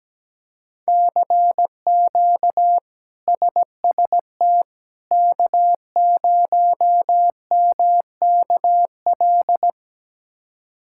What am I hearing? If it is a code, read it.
Morse code, heard as a keyed tone, about 17 wpm, 700 Hz. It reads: CQ SST K0MKL
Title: K1USN SST QSO Practice - 17wpm